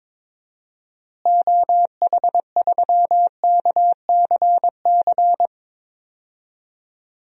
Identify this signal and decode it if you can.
Morse code, heard as a keyed tone, about 22 wpm, 700 Hz. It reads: OH3KCC
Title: DXCC-100 Call Signs - 6 Characters - Repeated 3x - 22wpm